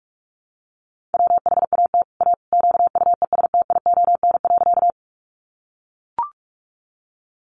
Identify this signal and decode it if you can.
Morse code, heard as a keyed tone, about 44 wpm, 700 Hz. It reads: WHAT A QUESTION.
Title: Sentences from Top 400 Words 45wpm